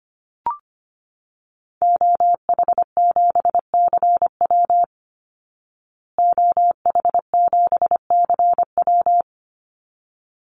Morse code, heard as a keyed tone, about 25 wpm, 700 Hz. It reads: OH7CW OH7CW
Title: Call Signs Repeated 3x - 25wpm